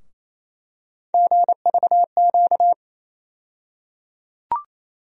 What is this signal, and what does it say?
Morse code, heard as a keyed tone, about 28 wpm, 700 Hz. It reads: GVQ
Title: USA Call Sign Postfixes - 3 Letters 28wpm